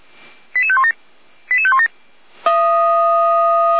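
Five-tone selective call (ZVEI), heard as quick runs of five tones, tones of about 70 ms. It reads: 88517 88517
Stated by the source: ZVEI five-tone selective call received by SDR.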